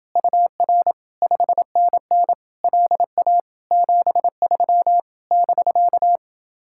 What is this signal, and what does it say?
Morse code, heard as a keyed tone, about 28 wpm, 700 Hz. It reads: UR 5NN LA 73 <BK>